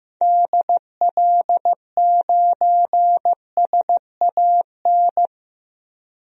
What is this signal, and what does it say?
Morse code, heard as a keyed tone, about 15 wpm, 700 Hz. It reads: DL9SAN
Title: DXCC-100 Call Signs - 6 Characters - Repeated 2x - 15wpm